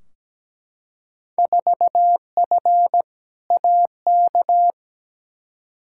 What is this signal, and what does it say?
Morse code, heard as a keyed tone, about 17 wpm, 700 Hz. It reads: 4F AK
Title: Field Day Exchanges Encore - 17wpm